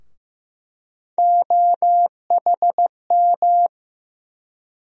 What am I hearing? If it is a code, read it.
Morse code, heard as a keyed tone, about 15 wpm, 700 Hz. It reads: OHM
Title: USA Call Sign Postfixes - 3 Letters 15wpm